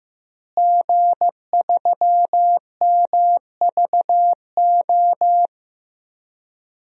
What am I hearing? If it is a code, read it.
Morse code, heard as a keyed tone, about 15 wpm, 700 Hz. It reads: G3MVO
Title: DXCC-100 Call Signs - 5 Characters - Repeated 2x - 15wpm